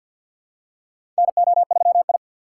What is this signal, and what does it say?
Morse code, heard as a keyed tone, about 53 wpm, 700 Hz. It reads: NO3I
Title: USA Call Signs - 4 Characters - 2x Character Spacing - 50wpm